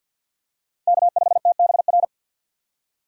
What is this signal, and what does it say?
Morse code, heard as a keyed tone, about 53 wpm, 690 Hz. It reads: K5TBR
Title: USA Call Signs - 5 Characters - 6x Character Spacing - 50wpm